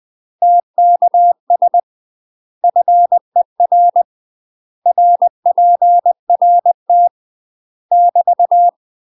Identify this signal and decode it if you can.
Morse code, heard as a keyed tone, about 20 wpm, 700 Hz. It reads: TKS FER RPRT <BT>
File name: MASTER QSOs TRAINING/20WPM